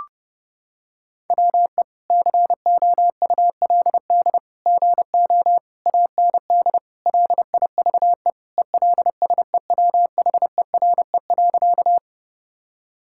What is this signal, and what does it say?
Morse code, heard as a keyed tone, about 30 wpm, 700 Hz. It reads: WE COULD GO AND LIVE ELSEWHERE.
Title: Sentences from Top 1600 Words 30wpm